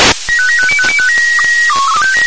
SSTV (slow-scan television), PD 120.
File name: ISS-SSTV